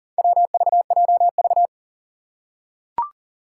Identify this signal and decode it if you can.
Morse code, heard as a keyed tone, about 40 wpm, 700 Hz. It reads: WVJV
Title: Four Challenging Letter-Number Combinations 40wpm